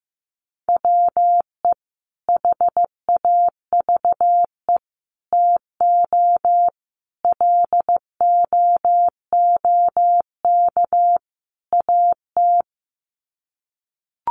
Morse code, heard as a keyed tone, about 15 wpm, 700 Hz. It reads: WE HAVE TO LOOK AT
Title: Sets of 5 Words Encore2 15wpm